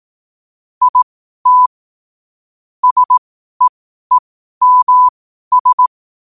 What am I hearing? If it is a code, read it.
Morse code, heard as a keyed tone, about 18 wpm, spaced as for 12 wpm, 1000 Hz. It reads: IT SEEMS
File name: 12wpm